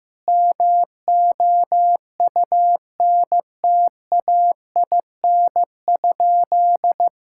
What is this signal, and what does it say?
Morse code, heard as a keyed tone, about 15 wpm, 700 Hz. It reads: MOUNTAIN?
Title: Sentences from Top 1000 Words - Review Part 5 - 15wpm